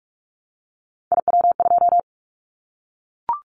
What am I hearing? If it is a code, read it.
Morse code, heard as a keyed tone, about 46 wpm, 700 Hz. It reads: IW2
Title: Callsign Prefixes for Top 25 DXCC - Repeated 3x - 45wpm